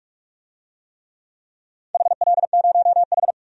Morse code, heard as a keyed tone, about 46 wpm, 700 Hz. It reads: HL0H
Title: DXCC-100 Call Signs - 4 Characters - 45wpm